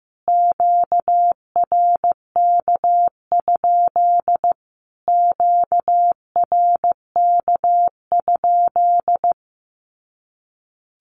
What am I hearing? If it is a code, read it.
Morse code, heard as a keyed tone, about 15 wpm, 700 Hz. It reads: QRK? QRK?
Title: Q-Codes Repeated 3x 15wpm